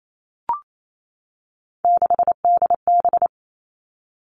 Morse code, heard as a keyed tone, about 28 wpm, 700 Hz. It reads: 6DB
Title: Three Challenging Letter-Number Combinations 28wpm